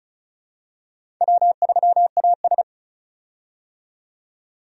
Morse code, heard as a keyed tone, about 34 wpm, 700 Hz. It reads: W3AS
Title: USA Call Signs - 4 Characters - 35wpm